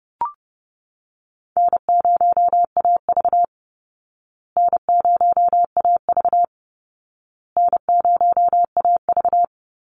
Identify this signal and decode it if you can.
Morse code, heard as a keyed tone, about 30 wpm, 700 Hz. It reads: N0AV N0AV N0AV
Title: Easy Call Signs Repeated 3x - 30wpm